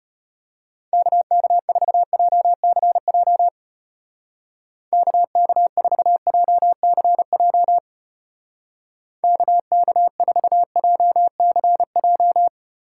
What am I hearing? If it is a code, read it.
Morse code, heard as a keyed tone, about 31 wpm, 700 Hz. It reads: KK4JCJ KK4JCJ KK4JCJ
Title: Speed Racing - USA 2x3 Call Signs 25wpm